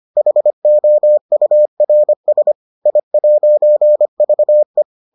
Morse code, heard as a keyed tone, about 25 wpm, 600 Hz. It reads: HOURS I'VE